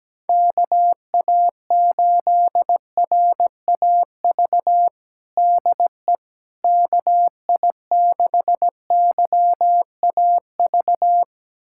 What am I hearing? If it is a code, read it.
Morse code, heard as a keyed tone, about 17 wpm, 700 Hz. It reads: KA8RAV DE KI6YAV
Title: USA Callsign DE Callsign - 6 Characters - 17wpm